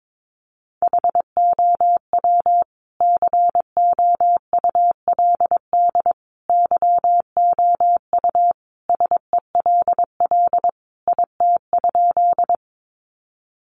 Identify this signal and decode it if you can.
Morse code, heard as a keyed tone, about 22 wpm, 700 Hz. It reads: HOW COULD YOU SELL IT?
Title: Sentences from Top 1100 Words 22wpm